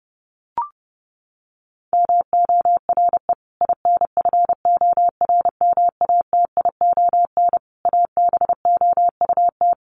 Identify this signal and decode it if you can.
Morse code, heard as a keyed tone, about 30 wpm, 700 Hz. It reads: MORE INFORMATION ABOUT
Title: Sentences from Top 1200 Words 30wpm